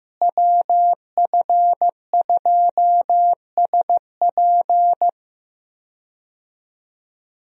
Morse code, heard as a keyed tone, about 15 wpm, 700 Hz. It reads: WF2SP
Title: USA Call Signs - 5 Characters - Repeated 3x - 15wpm